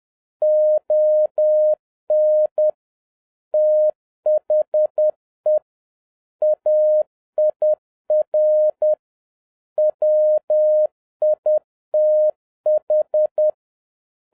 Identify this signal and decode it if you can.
Morse code, heard as a keyed tone, about 10 wpm, 610 Hz. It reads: ON THE AIR WITH
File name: 2011-07-mp3 / n36 THE 3.11 2 10wpm